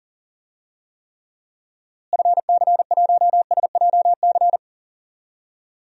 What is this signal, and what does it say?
Morse code, heard as a keyed tone, about 40 wpm, 700 Hz. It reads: FC1SJC
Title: DXCC-100 Call Signs - 6 Characters - Repeated 3x - 40wpm